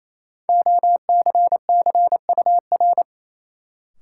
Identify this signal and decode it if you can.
Morse code, heard as a keyed tone, about 28 wpm, 700 Hz. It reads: OCCUR
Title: Sentences from Top 900 Words - New Word Drill - 28wpm